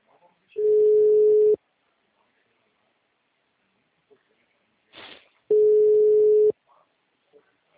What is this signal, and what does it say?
Ringback tone (Europe, 425 Hz).